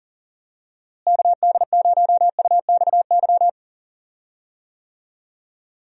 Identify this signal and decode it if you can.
Morse code, heard as a keyed tone, about 40 wpm, 700 Hz. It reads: KD0UXY
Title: USA Call Signs - 6 Characters - Repeated 3x - 40wpm